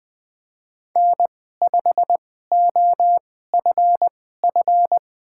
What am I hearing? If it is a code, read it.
Morse code, heard as a keyed tone, about 20 wpm, spaced as for 14 wpm, 700 Hz. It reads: N5OFF
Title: USA Call Signs - 5 Characters - 2x Character Spacing - 20wpm